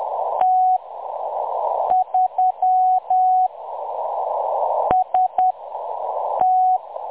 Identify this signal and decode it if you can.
Morse code, heard as a keyed tone, about 10 wpm, spaced as for 6 wpm, 740 Hz. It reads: T3 ST
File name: M23-ST3- -0858utc-10184khz